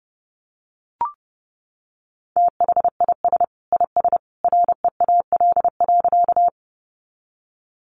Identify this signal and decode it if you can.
Morse code, heard as a keyed tone, about 30 wpm, 700 Hz. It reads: THIS IS REAL.
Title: Sentences from Top 300 Words 30wpm